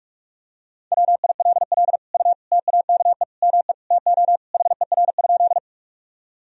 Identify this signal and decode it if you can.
Morse code, heard as a keyed tone, about 41 wpm, 700 Hz. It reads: WILL U TAKE ME TO HER?
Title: Sentences from Top 200 Words and Abbreviations 45wpm